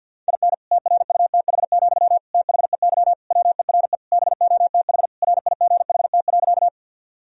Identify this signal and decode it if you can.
Morse code, heard as a keyed tone, about 52 wpm, 700 Hz. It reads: IN TRUTH, THEY WERE BOTH RIGHT.
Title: Sentences from Top 1000 Words - Review Part 1 - 50wpm